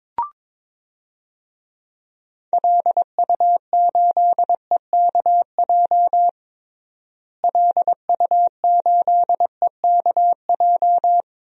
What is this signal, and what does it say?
Morse code, heard as a keyed tone, about 22 wpm, 700 Hz. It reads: LU8EKJ LU8EKJ
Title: DXCC-100 Call Signs - 6 Characters - Repeated 3x - 22wpm